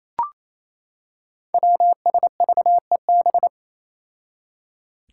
Morse code, heard as a keyed tone, about 27 wpm, 700 Hz. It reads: WSVEB
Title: Five Challenging Letter-Number Combinations 28wpm